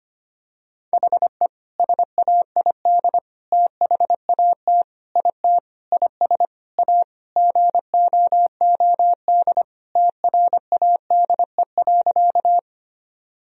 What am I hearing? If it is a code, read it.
Morse code, heard as a keyed tone, about 25 wpm, 700 Hz. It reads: HE SAID THAT IT IS A GOOD TRADE.
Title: Sentences from Top 1000 Words - Review Part 4 - 25wpm